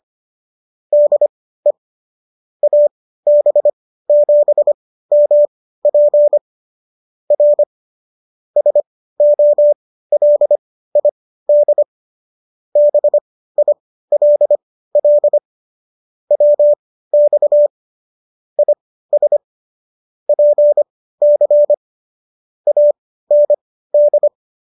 Morse code, heard as a keyed tone, about 25 wpm, spaced as for 15 wpm, 590 Hz. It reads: DE AB7MP R SOLID BILL WX IS PC AND